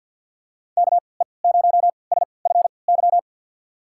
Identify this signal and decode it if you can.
Morse code, heard as a keyed tone, about 52 wpm, spaced as for 29 wpm, 700 Hz. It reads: KE0SFY
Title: USA Call Signs - 6 Characters - 3x Character Spacing - 50wpm